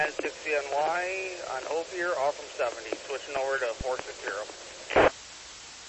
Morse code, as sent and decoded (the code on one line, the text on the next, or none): none